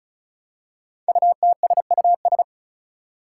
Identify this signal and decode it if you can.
Morse code, heard as a keyed tone, about 36 wpm, 700 Hz. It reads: UTSUS
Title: Five Challenging Letter-Number Combinations 35wpm